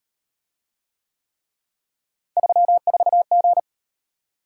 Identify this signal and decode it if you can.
Morse code, heard as a keyed tone, about 38 wpm, 700 Hz. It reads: 34G